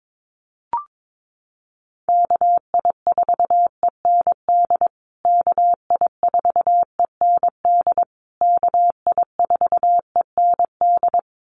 Morse code, heard as a keyed tone, about 22 wpm, 700 Hz. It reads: KI4END KI4END KI4END
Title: USA Call Signs Repeated 3x 22wpm